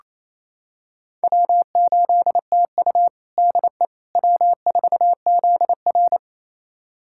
Morse code, heard as a keyed tone, about 27 wpm, 700 Hz. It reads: W8TU DE W4ZR
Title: USA Callsign DE Callsign - 4 Characters - 28wpm